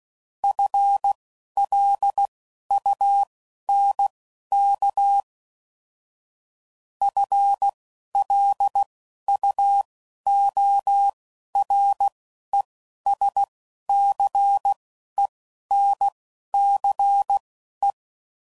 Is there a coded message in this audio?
Yes: Morse code from 0.4 to 17.9 s, heard as a keyed tone, about 16 wpm, spaced as for 11 wpm, 800 Hz. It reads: FLUNK FLUORESCENCE